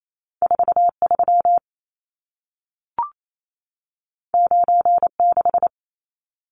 Morse code, heard as a keyed tone, about 28 wpm, 700 Hz. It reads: 43 96
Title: Numbers - 2-Digit 28wpm